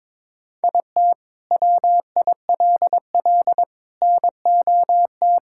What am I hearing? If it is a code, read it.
Morse code, heard as a keyed tone, about 22 wpm, 700 Hz. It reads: IT WILL NOT
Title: Sentences from Top 1000 Words - Review Part 6 - 22wpm